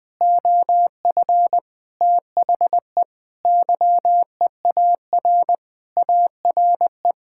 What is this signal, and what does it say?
Morse code, heard as a keyed tone, about 21 wpm, 700 Hz. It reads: OF THE YEAR ARE
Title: Sentences with 7-Word Phrases - Spoken Beginning - 20wpm